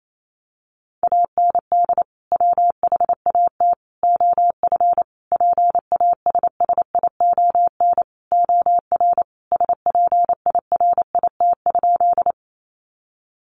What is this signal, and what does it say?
Morse code, heard as a keyed tone, about 28 wpm, 700 Hz. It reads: AND WHAT OF PASSION OR SPIRIT?
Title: Sentences from Top 1400 Words 28wpm